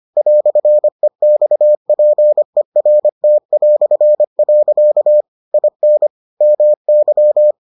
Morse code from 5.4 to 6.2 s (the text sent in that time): IN